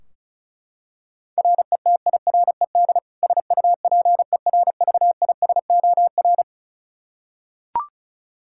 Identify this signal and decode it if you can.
Morse code, heard as a keyed tone, about 34 wpm, 700 Hz. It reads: RETIRED SUPERVISOR
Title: K1USN SST QSO Practice 35wpm